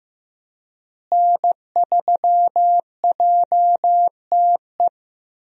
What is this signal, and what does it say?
Morse code, heard as a keyed tone, about 15 wpm, 700 Hz. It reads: N3JTE
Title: USA Call Signs - 5 Characters - Repeated 3x - 15wpm